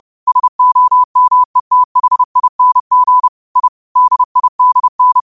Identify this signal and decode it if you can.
Morse code, heard as a keyed tone, about 30 wpm, 1000 Hz. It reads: SOMETHING I DIDN
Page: Quotes for Fri, 15 Aug 2025 in Morse Code at 30 words per minute.